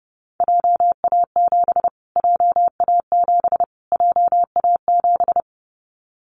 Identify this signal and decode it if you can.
Morse code, heard as a keyed tone, about 30 wpm, 700 Hz. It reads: JA7 JA7 JA7
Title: Callsign Prefixes for Top 25 DXCC - Repeated 3x - 30wpm